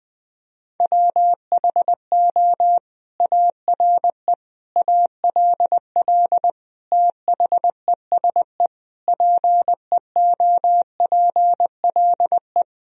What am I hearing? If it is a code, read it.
Morse code, heard as a keyed tone, about 21 wpm, 700 Hz. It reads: WHO ARE ALL THESE PEOPLE
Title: Sentences from Top 1000 Words - Review Part 1 - 20wpm